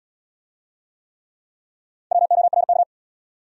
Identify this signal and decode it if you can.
Morse code, heard as a keyed tone, about 66 wpm, 700 Hz. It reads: WORK